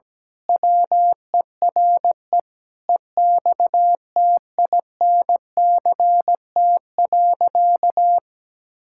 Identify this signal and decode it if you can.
Morse code, heard as a keyed tone, about 17 wpm, 700 Hz. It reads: WERE EXTINCT.
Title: Pangram Challenge v2 - 17wpm